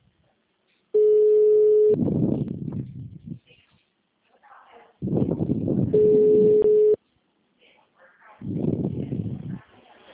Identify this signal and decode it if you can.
Ringback tone (Europe, 425 Hz), with speech.
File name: out- -2031- - - .17649